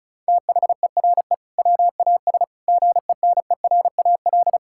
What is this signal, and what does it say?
Morse code, heard as a keyed tone, about 36 wpm, 690 Hz. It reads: THERE WAS GENERAL
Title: Sentences from Top 1600 Words 35wpm